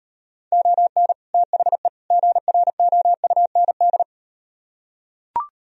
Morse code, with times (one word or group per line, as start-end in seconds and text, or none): 0.5-1.1 ON
1.3-1.9 THE
2.1-4.0 GROUND